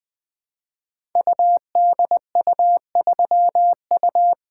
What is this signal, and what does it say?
Morse code, heard as a keyed tone, about 20 wpm, 700 Hz. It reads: UDU3U
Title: Code Groups - 5 Letter:Numbers - 20wpm